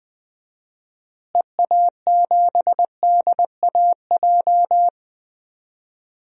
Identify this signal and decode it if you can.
Morse code, heard as a keyed tone, about 20 wpm, 700 Hz. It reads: EA7DAJ
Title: DXCC-100 Call Signs - 6 Characters - Repeated 2x - 20wpm